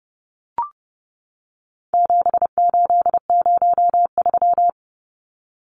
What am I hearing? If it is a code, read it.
Morse code, heard as a keyed tone, about 30 wpm, 700 Hz. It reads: 7803
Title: Numbers - 4-Digit 30wpm